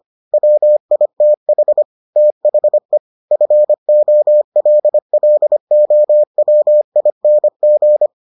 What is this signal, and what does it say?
Morse code, heard as a keyed tone, about 25 wpm, 600 Hz. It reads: WITH THE FOLLOWING